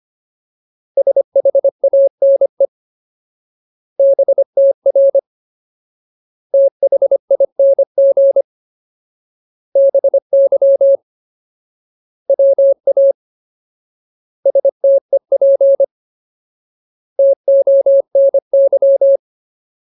Morse code, heard as a keyed tone, about 25 wpm, 550 Hz. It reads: SHANE BTR THING BY WA STEP TONY
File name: Ultimate Word List / 25WPM Ultimate